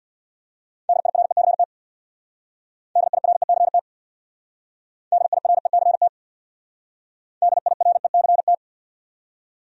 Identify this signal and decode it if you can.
Morse code, heard as a keyed tone, about 62 wpm, 700 Hz. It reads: DIRECT DIRECT DIRECT DIRECT
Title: Speed-Racing Top 500 Words 50wpm